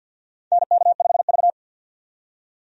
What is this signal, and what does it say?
Morse code, heard as a keyed tone, about 52 wpm, 700 Hz. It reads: NK5V
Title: USA Call Signs - 4 Characters - 50wpm